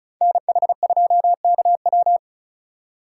Morse code, heard as a keyed tone, about 36 wpm, 700 Hz. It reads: NH2KW
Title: USA Call Signs - 5 Characters - 2x Character Spacing - 35wpm